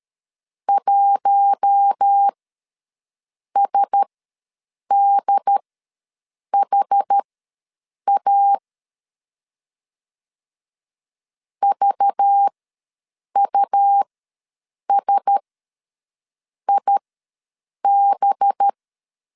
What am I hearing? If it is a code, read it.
Morse code, heard as a keyed tone, about 13 wpm, spaced as for 7 wpm, 800 Hz. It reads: 1SDHA VUSIB
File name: LEZIONI / 5 LBVC / 5 LBVC ws 12wpm 3159